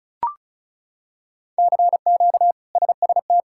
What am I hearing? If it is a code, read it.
Morse code, heard as a keyed tone, about 36 wpm, 700 Hz. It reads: CQ SST
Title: K1USN SST QSO Practice - 35wpm